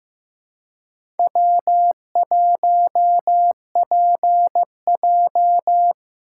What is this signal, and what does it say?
Morse code, heard as a keyed tone, about 15 wpm, 700 Hz. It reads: W1PJ
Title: USA Call Signs - 4 Characters - 2x Character Spacing - 15wpm